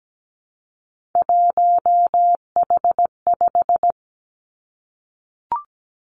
Morse code, heard as a keyed tone, about 17 wpm, 700 Hz. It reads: 1H5
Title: Three Challenging Letter-Number Combinations 17wpm